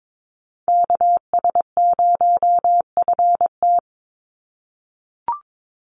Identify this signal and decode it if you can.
Morse code, heard as a keyed tone, about 22 wpm, 700 Hz. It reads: KS0FT
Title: USA Call Signs 22wpm